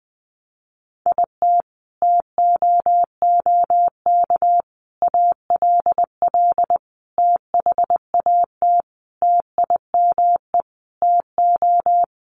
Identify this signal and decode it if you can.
Morse code, heard as a keyed tone, about 20 wpm, 700 Hz. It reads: IT TOOK ALL THAT TIME TO
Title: Sentences from Top 300 Words 20wpm